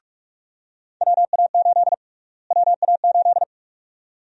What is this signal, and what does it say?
Morse code, heard as a keyed tone, about 45 wpm, 700 Hz. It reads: WA8 WA8
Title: USA Call Sign Prefixes - Repeated 2x - 45wpm